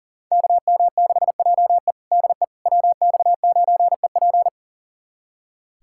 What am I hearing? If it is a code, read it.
Morse code, heard as a keyed tone, about 37 wpm, 700 Hz. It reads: KM6JE DE WX9EP
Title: USA Callsign DE Callsign 40wpm